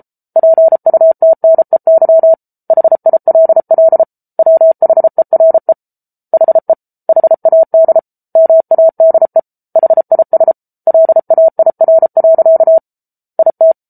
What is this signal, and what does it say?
Morse code, heard as a keyed tone, about 34 wpm, 650 Hz. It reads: PUTNEY HILL WHERE HE HAD MADE HIS LAIR. IT